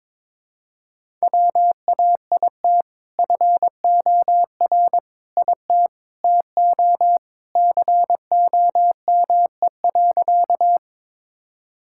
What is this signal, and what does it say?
Morse code, heard as a keyed tone, about 22 wpm, 700 Hz. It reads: WAIT FOR IT TO COME.
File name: practice / 22wpm / Sentences with 2-Word Phrases - 22wpm